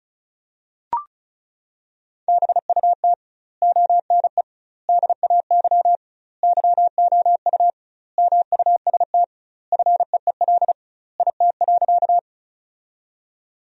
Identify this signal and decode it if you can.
Morse code, heard as a keyed tone, about 34 wpm, 700 Hz. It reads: BUT ONE DAY YOU MUST FEEL IT.
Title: 2x Word-Spacing - Sentences from Top 400 Words - 35wpm